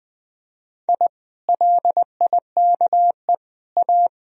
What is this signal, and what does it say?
Morse code, heard as a keyed tone, about 20 wpm, 700 Hz. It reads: I LIKE A
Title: Sentences from Top 1000 Words - Review Part 2 - 20wpm